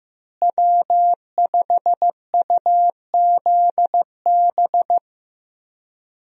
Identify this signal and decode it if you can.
Morse code, heard as a keyed tone, about 15 wpm, 700 Hz. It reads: W5UZB
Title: USA Call Signs - 5 Characters - Repeated 3x - 15wpm